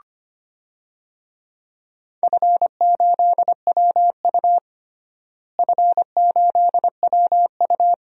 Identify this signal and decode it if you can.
Morse code, heard as a keyed tone, about 25 wpm, 700 Hz. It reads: F8WU F8WU
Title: DXCC-100 Call Signs - 4 Characters - Repeated 2x - 25wpm